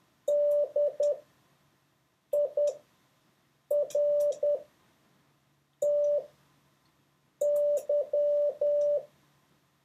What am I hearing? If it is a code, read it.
Morse code, heard as a keyed tone, about 10 wpm, spaced as for 5 wpm, 580 Hz. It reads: DIRTY